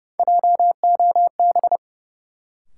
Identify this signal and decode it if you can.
Morse code, heard as a keyed tone, about 32 wpm, 700 Hz. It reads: JOB
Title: Sentences from Top 600 Words - New Word Drill - 30wpm